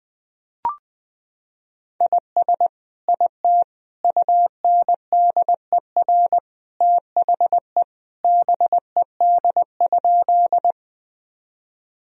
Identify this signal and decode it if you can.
Morse code, heard as a keyed tone, about 20 wpm, 700 Hz. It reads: IS IT UNDER THE BED?
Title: Sentences from Top 1000 Words - Review Part 1 - 20wpm